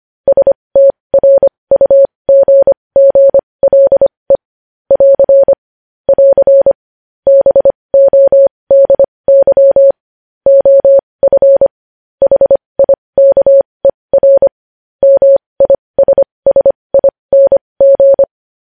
Morse code, heard as a keyed tone, about 25 wpm, spaced as for 20 wpm, 550 Hz. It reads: STRUGGLE <AR> <AR> BODY OF HIKER MISSING
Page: news-headlines-in-morse-code-at-20-wpm